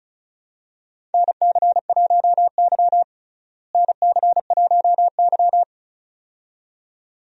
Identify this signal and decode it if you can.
Morse code, heard as a keyed tone, about 34 wpm, 700 Hz. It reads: NC1Y NC1Y
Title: USA Call Signs - 4 Characters - Repeated 2x - 35wpm